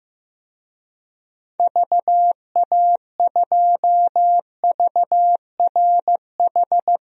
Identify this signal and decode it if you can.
Morse code, heard as a keyed tone, about 15 wpm, 700 Hz. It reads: VA2VRH
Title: DXCC-100 Call Signs - 6 Characters - Repeated 3x - 15wpm